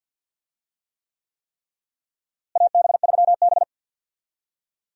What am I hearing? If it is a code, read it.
Morse code, heard as a keyed tone, about 48 wpm, 700 Hz. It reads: AB3B
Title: USA Call Signs - 4 Characters - Repeated 3x - 50wpm